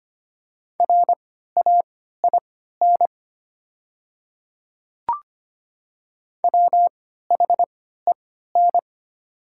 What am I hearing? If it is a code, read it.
Morse code, heard as a keyed tone, about 25 wpm, spaced as for 14 wpm, 700 Hz. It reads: RAIN WHEN
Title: 3x Character Spacing - Top 300 Words - 25wpm